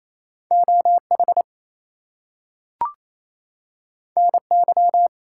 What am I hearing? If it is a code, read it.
Morse code, heard as a keyed tone, about 28 wpm, 700 Hz. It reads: OH NY
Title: US States and Canadian Provinces 28wpm